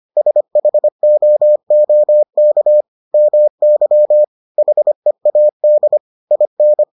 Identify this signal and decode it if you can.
Morse code, heard as a keyed tone, about 25 wpm, 600 Hz. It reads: SHOOK MY HEAD IN